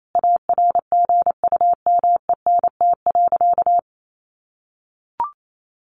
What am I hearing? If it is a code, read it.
Morse code, heard as a keyed tone, about 28 wpm, 700 Hz. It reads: ARGUMENT.
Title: Sentences from Top 1400 Words 28wpm